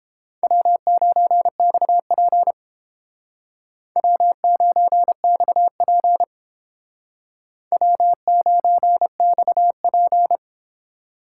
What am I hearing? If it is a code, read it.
Morse code, heard as a keyed tone, about 29 wpm, 700 Hz. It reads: W9XP W9XP W9XP